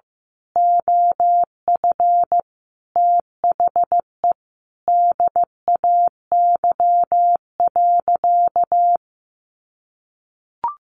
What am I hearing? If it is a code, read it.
Morse code, heard as a keyed tone, about 15 wpm, 700 Hz. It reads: OF THE DAY.
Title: Sentences from Top 1400 Words 15wpm